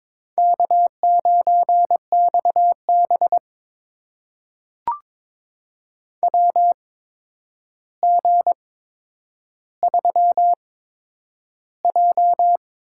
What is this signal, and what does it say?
Morse code, heard as a keyed tone, about 22 wpm, spaced as for 9 wpm, 700 Hz. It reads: K9XB WG3J
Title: USA Call Signs - 4 Characters - 8x Character Spacing - 22wpm